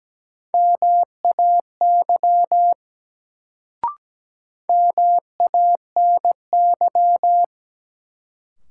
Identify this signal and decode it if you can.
Morse code, heard as a keyed tone, about 17 wpm, 700 Hz. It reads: MAY MANY
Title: Sets of 2 Words - New Word Drill - 17wpm